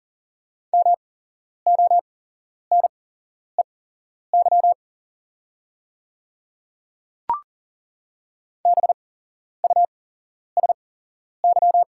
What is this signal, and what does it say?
Morse code, heard as a keyed tone, about 40 wpm, spaced as for 11 wpm, 690 Hz. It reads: MONEY BUSY
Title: 8x Character Spacing - Top 500 Words - 40wpm